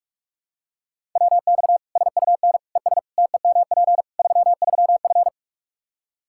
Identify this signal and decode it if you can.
Morse code, heard as a keyed tone, about 42 wpm, 700 Hz. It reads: WX SUN ES TEMP 33F
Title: K1USN SST QSO Practice 45wpm